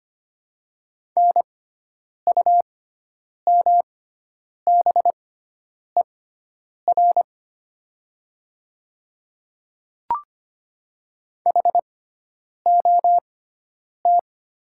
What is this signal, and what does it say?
Morse code, heard as a keyed tone, about 25 wpm, spaced as for 9 wpm, 700 Hz. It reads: NUMBER HOT
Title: 6x Character Spacing - Top 100 Words - 25wpm